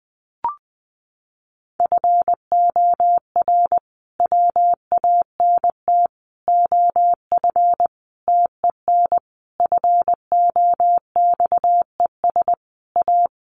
Morse code, heard as a keyed tone, about 20 wpm, 700 Hz. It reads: FOR WANT OF TEN FOXES A
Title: A Sound Of Thunder 20wpm